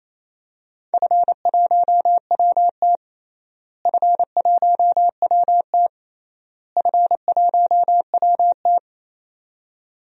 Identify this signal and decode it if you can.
Morse code, heard as a keyed tone, about 27 wpm, 700 Hz. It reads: F1WT F1WT F1WT
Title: DXCC-100 Call Signs - 4 Characters - Repeated 3x - 28wpm